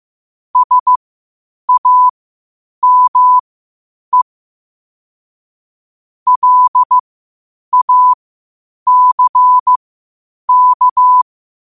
Morse code, heard as a keyed tone, about 15 wpm, spaced as for 8 wpm, 1000 Hz. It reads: SAME LACK
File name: qotdfiles / 8wpm